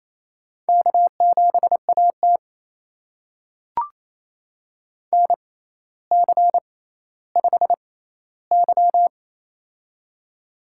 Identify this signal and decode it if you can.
Morse code, heard as a keyed tone, about 28 wpm, spaced as for 14 wpm, 700 Hz. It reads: K7AT NC5Y
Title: USA Call Signs - 4 Characters - 6x Character Spacing - 28wpm